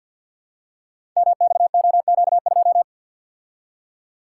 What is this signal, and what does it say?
Morse code, heard as a keyed tone, about 50 wpm, 700 Hz. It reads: MXOQ2
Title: Code Groups - 5 Letter:Numbers - 50wpm